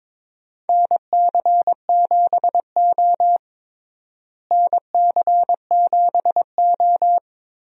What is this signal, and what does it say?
Morse code, heard as a keyed tone, about 21 wpm, 700 Hz. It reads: NC7O NC7O
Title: USA Call Signs - 4 Characters - Repeated 3x - 22wpm